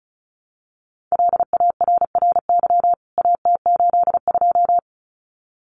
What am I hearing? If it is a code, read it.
Morse code, heard as a keyed tone, about 34 wpm, 700 Hz. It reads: LARRY AT82
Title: CWT Exchange 35wpm